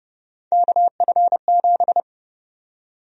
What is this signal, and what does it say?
Morse code, heard as a keyed tone, about 30 wpm, 700 Hz. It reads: KF7
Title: USA Call Sign Prefixes - Repeated 2x - 30wpm